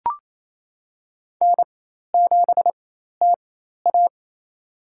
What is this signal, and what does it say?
Morse code, heard as a keyed tone, about 28 wpm, spaced as for 13 wpm, 700 Hz. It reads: N7TA
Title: USA Call Signs - 4 Characters - 4x Character Spacing - 28wpm